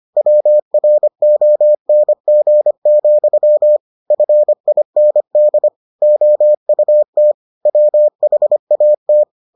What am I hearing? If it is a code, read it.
Morse code, heard as a keyed tone, about 25 wpm, 600 Hz. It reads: WRONG, FIND OUT WHAT